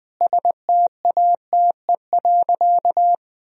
Morse code, heard as a keyed tone, about 20 wpm, 700 Hz. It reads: STATE.
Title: Sentences from Top 1000 Words - Review Part 4 - 20wpm